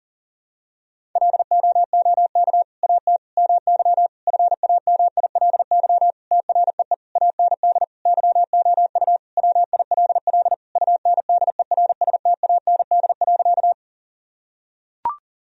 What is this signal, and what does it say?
Morse code, heard as a keyed tone, about 40 wpm, 700 Hz. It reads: LOOK AT MY FAMILY TREE AND YOU WILL UNDERSTAND.
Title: Sentences from Top 1000 Words - Review Part 1 - 40wpm